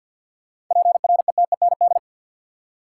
Morse code, heard as a keyed tone, about 50 wpm, 700 Hz. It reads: PRETEND
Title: Seven Letter Words - 50wpm